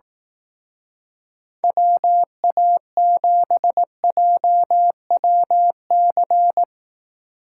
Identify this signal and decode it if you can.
Morse code, heard as a keyed tone, about 18 wpm, 700 Hz. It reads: WA7JWC
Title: Speed Racing - USA 2x3 Call Signs 15wpm